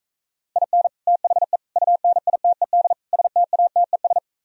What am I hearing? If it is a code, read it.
Morse code, heard as a keyed tone, about 45 wpm, 700 Hz. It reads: IN THE UNITED STATES